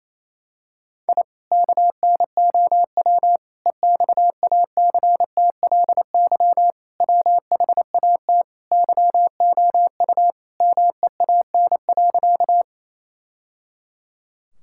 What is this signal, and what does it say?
Morse code, heard as a keyed tone, about 28 wpm, 700 Hz. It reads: I KNOW EXACTLY WHAT YOU MEAN.
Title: Sentences from Top 1000 Words - Review Part 6 - 28wpm